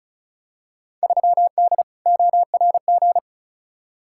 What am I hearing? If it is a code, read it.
Morse code, heard as a keyed tone, about 36 wpm, 700 Hz. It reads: 3D ORG
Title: Field Day Exchanges Encore - 35wpm